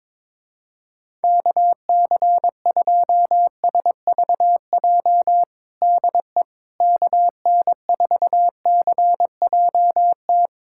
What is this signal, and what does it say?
Morse code, heard as a keyed tone, about 22 wpm, 700 Hz. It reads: KC2SVJ DE KN4CJT
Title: USA Callsign DE Callsign - 6 Characters - 22wpm